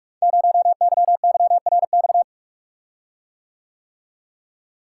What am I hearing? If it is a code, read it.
Morse code, heard as a keyed tone, about 46 wpm, 690 Hz. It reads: 0YYRX
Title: Code Groups - 5 Letter:Numbers - 45wpm